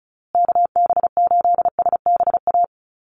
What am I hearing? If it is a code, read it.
Morse code, heard as a keyed tone, about 34 wpm, 700 Hz. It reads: KB8SBA
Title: USA Call Signs Repeated 3x 35wpm